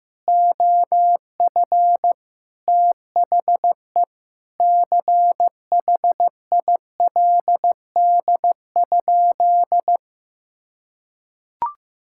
Morse code, heard as a keyed tone, about 15 wpm, 700 Hz. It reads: OF THE CHILD?
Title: Sentences from Top 1600 Words 15wpm